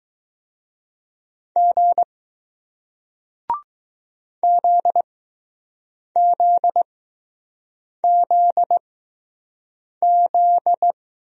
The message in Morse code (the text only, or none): G ZZZZ